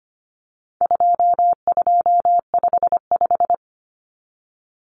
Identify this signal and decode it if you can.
Morse code, heard as a keyed tone, about 25 wpm, 700 Hz. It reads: 2255
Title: Numbers - 4-Digit 25wpm